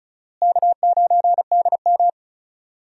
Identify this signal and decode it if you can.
Morse code, heard as a keyed tone, about 36 wpm, 700 Hz. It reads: K9DM
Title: USA Call Signs - 4 Characters - Repeated 3x - 35wpm